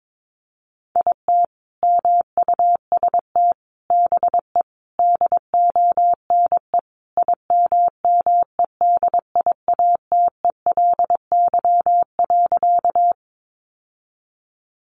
Morse code, heard as a keyed tone, about 22 wpm, 700 Hz. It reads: IT MUST BE DONE IMMEDIATELY.
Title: Sentences from Top 1100 Words 22wpm